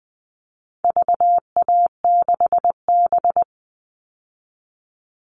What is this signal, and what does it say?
Morse code, heard as a keyed tone, about 20 wpm, 700 Hz. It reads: VA6B
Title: All Cut Numbers - 4-Digit 20wpm